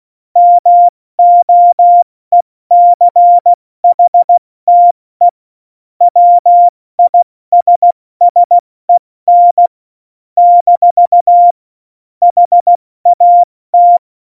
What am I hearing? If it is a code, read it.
Morse code, heard as a keyed tone, about 16 wpm, 700 Hz. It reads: MOECHTE WISSEN - HAT
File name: Morsekurs / Witze / Witz 24. November 2020, 70 F80